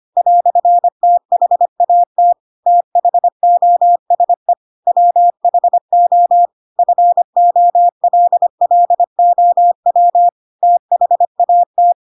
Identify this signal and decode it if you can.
Morse code, heard as a keyed tone, about 25 wpm, 700 Hz. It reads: "THAT THOSE WHO FOLLOW THAT
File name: Pinocchio / Pinocchio ch4 25 25 ik1hge